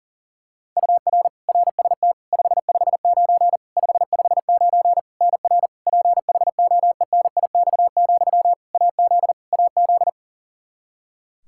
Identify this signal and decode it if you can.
Morse code, heard as a keyed tone, about 40 wpm, 700 Hz. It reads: UR RST 559 559 NR PHOENIX, AZ AZ
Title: Practice QSO - Exchange 1 - Beginning - 40wpm